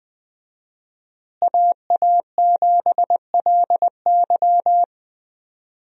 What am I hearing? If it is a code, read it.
Morse code, heard as a keyed tone, about 20 wpm, 700 Hz. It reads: AA7LY
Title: USA Call Signs - 5 Characters - Repeated 3x - 20wpm